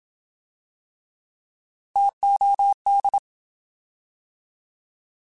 Morse code, heard as a keyed tone, about 27 wpm, 800 Hz. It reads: TOD
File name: Alphabetical / Alphabetical mp3s / Ts @25